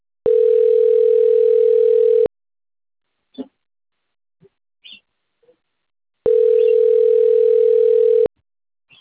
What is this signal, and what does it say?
Ringback tone (North America), 440 + 480 Hz.